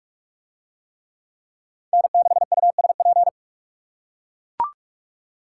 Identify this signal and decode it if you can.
Morse code, heard as a keyed tone, about 41 wpm, 700 Hz. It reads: N6USP